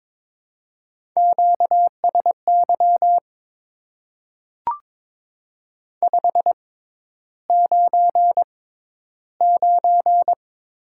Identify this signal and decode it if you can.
Morse code, heard as a keyed tone, about 22 wpm, spaced as for 11 wpm, 700 Hz. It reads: QSY 599
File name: Top 90 QSO Elements - 6x Character Spacing - 22wpm